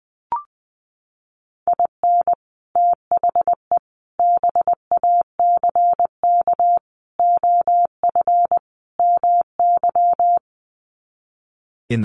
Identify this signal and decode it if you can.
Morse code, heard as a keyed tone, about 19 wpm, 700 Hz. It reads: IN THE BACK OF MY
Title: Sets of 5 Words Encore 20wpm